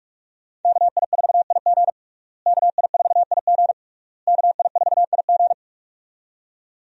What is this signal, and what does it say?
Morse code, heard as a keyed tone, about 46 wpm, 700 Hz. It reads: KI4IG KI4IG KI4IG
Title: USA Call Signs - 5 Characters - Repeated 3x - 45wpm